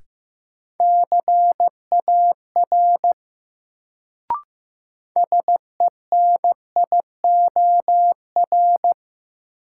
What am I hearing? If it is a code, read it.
Morse code, heard as a keyed tone, about 15 wpm, 700 Hz. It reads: CAR SENIOR
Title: Sets of 5 Words Encore - New Word Drill - 15wpm